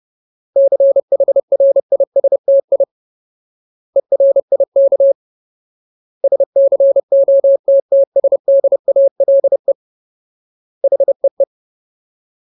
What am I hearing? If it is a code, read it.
Morse code, heard as a keyed tone, about 30 wpm, 550 Hz. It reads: CHRISTI ERIK SCOTTSDALE HEE
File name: Ultimate Word List / 30WPM Ultimate